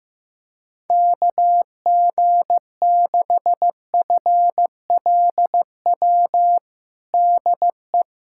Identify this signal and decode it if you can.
Morse code, heard as a keyed tone, about 15 wpm, 700 Hz. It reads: KG6FLW DE
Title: USA Callsign DE Callsign 15wpm